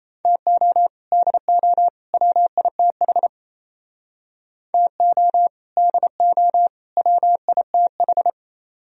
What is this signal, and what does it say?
Morse code, heard as a keyed tone, about 34 wpm, 700 Hz. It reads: TO DO WITH TO DO WITH
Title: Speed-Racing - Sets of 3 Words 28wpm